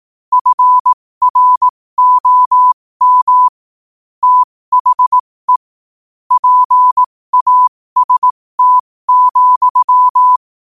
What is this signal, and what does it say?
Morse code, heard as a keyed tone, about 18 wpm, spaced as for 15 wpm, 1000 Hz. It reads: FROM THE PAST,